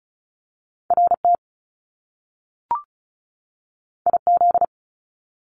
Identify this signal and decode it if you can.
Morse code, heard as a keyed tone, about 36 wpm, 700 Hz. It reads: RT IZ
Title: Top Two Letter Combinations in Words 35wpm